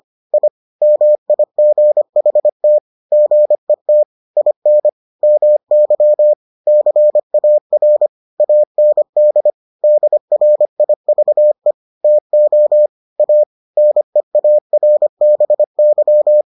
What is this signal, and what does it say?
Morse code, heard as a keyed tone, about 25 wpm, 600 Hz. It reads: I MIGHT GET IN MY CAR AND DRIVE TO A NEARBY